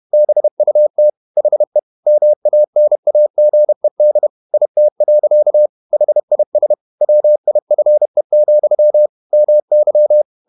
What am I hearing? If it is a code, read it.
Morse code, heard as a keyed tone, about 31 wpm, 600 Hz. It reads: BUT HE MANAGED IT. HIS WIFE, MY
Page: Never 31wpm | CW med Gnister